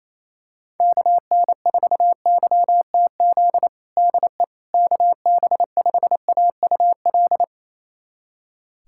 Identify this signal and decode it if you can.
Morse code, heard as a keyed tone, about 28 wpm, 700 Hz. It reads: KN4YTZ DE KB5AUL
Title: USA Callsign DE Callsign - 6 Characters - 28wpm